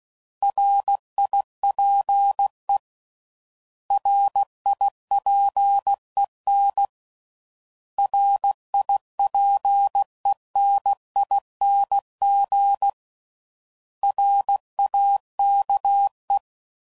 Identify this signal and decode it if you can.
Morse code, heard as a keyed tone, about 16 wpm, 800 Hz. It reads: RIPE RIPEN RIPENING RAKE
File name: Prefixes / By First Letter / Rs